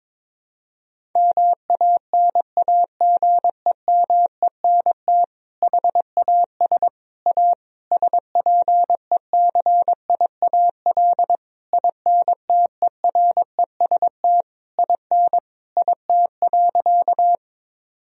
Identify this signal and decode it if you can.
Morse code, heard as a keyed tone, about 22 wpm, 700 Hz. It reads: MANAGEMENT HAS A SPECIAL INTEREST IN IT.
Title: Sentences from Top 1000 Words - Review Part 4 - 22wpm